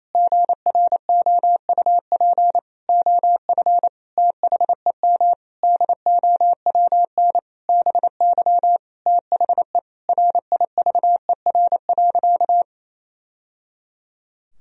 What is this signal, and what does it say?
Morse code, heard as a keyed tone, about 28 wpm, 700 Hz. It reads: GROUP OF THEM DOWN BY THE RIVER.
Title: Sentences from Top 1000 Words - Review Part 1 - 28wpm